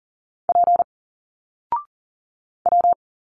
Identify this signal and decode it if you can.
Morse code, heard as a keyed tone, about 40 wpm, 700 Hz. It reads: P W
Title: Single Letters 40wpm